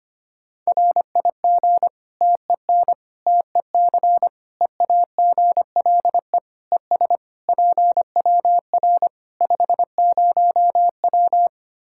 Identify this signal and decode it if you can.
Morse code, heard as a keyed tone, about 25 wpm, 700 Hz. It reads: RIG TEN TEC EAGLE ES PWR 50W
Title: Practice QSO - Exchange 2 - Info - 25wpm